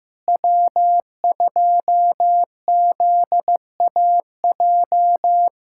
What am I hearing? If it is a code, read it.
Morse code, heard as a keyed tone, about 15 wpm, 700 Hz. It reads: W2ZAJ
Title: USA Callsign DE Callsign - 5 Characters - 15wpm